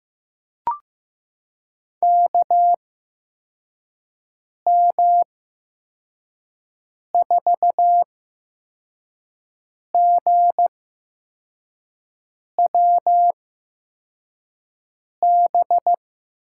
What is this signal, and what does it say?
Morse code, heard as a keyed tone, about 15 wpm, spaced as for 4 wpm, 700 Hz. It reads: KM4GWB
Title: USA Call Signs - 6 Characters - 8x Character Spacing - 15wpm